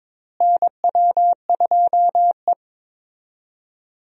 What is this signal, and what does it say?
Morse code, heard as a keyed tone, about 21 wpm, 700 Hz. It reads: NW2E
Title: USA Call Signs - 4 Characters - Repeated 3x - 22wpm